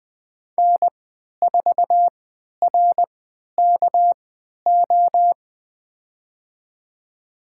Morse code, heard as a keyed tone, about 21 wpm, spaced as for 12 wpm, 700 Hz. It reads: N4RKO
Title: USA Call Signs - 5 Characters - 3x Character Spacing - 20wpm